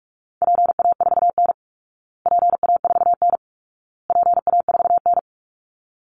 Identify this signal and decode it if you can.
Morse code, heard as a keyed tone, about 44 wpm, 700 Hz. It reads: PA4N PA4N PA4N
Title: Call Signs Repeated 2x - 45wpm